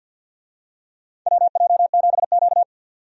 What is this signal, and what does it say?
Morse code, heard as a keyed tone, about 50 wpm, 700 Hz. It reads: WJ7Q
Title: USA Call Signs - 4 Characters - 50wpm